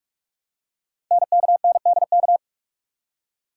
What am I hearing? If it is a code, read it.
Morse code, heard as a keyed tone, about 44 wpm, 700 Hz. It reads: NKNDK
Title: Code Groups - 5 Letters - 45wpm